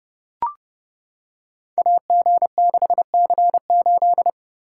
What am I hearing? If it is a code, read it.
Morse code, heard as a keyed tone, about 32 wpm, 700 Hz. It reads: AG6C8
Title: Code Groups - 5 Letter:Numbers - 30wpm